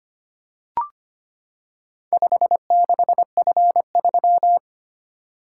Morse code, heard as a keyed tone, about 25 wpm, 700 Hz. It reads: 56F3
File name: practice / 25wpm / Four Letter-Number Combinations - 25wpm